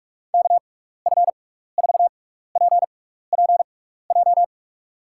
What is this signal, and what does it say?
Morse code, heard as a keyed tone, about 47 wpm, spaced as for 16 wpm, 700 Hz. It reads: KF4PPJ
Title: USA Call Signs - 6x Character Spacing - 45wpm